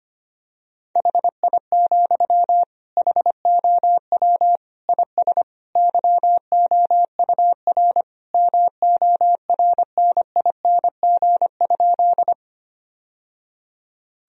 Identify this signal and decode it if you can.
Morse code, heard as a keyed tone, about 25 wpm, 700 Hz. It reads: HI, HOW IS YOUR MORNING?
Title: Sentences from Top 1000 Words - Review Part 3 - 25wpm